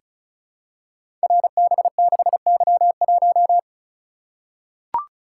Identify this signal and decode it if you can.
Morse code, heard as a keyed tone, about 36 wpm, 700 Hz. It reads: RB6Y1
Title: Code Groups - 5 Letter:Numbers - 35wpm